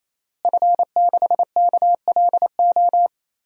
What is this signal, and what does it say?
Morse code, heard as a keyed tone, about 28 wpm, 700 Hz. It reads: F6KLO
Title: DXCC-100 Call Signs - 5 Characters - Repeated 3x - 28wpm